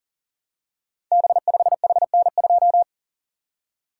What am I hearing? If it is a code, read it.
Morse code, heard as a keyed tone, about 39 wpm, 700 Hz. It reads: B5HN2
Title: Code Groups - 5 Letter:Numbers - 40wpm